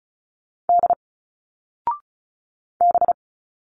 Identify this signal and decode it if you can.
Morse code, heard as a keyed tone, about 36 wpm, 700 Hz. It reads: D B
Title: Single Letters 35wpm